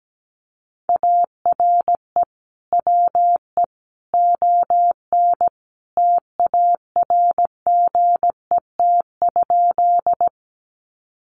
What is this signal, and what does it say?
Morse code, heard as a keyed tone, about 17 wpm, 700 Hz. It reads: ARE WE ON TARGET?
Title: Sentences from Top 1200 Words 17wpm